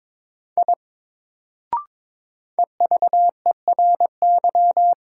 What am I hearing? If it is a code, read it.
Morse code, heard as a keyed tone, about 21 wpm, 700 Hz. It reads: I EVERY
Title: Sentences from Top 200 Words - New Word Drill - 22wpm